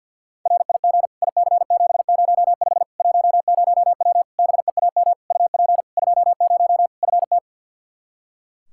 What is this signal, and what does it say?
Morse code, heard as a keyed tone, about 52 wpm, 700 Hz. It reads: RIG IC705 10W BEAM UP 20 FT
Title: Practice QSO - Exchange 2 - Info - 50wpm